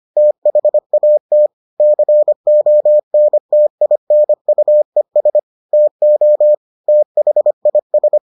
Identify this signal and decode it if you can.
Morse code, heard as a keyed tone, about 25 wpm, 600 Hz. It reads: THAT CONTINUES TO THIS